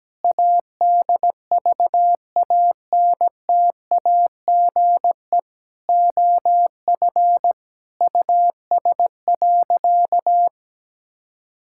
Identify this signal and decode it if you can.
Morse code, heard as a keyed tone, about 17 wpm, 700 Hz. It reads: ADVANTAGE OF US.
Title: Sentences from Top 1000 Words - Review Part 6 - 17wpm